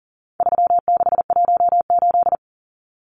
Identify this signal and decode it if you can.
Morse code, heard as a keyed tone, about 40 wpm, 700 Hz. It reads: 3618
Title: Numbers - 4-Digit 40wpm